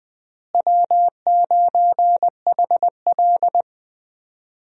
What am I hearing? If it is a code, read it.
Morse code, heard as a keyed tone, about 20 wpm, 700 Hz. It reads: W9HL
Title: USA Call Signs - 4 Characters - Repeated 2x - 20wpm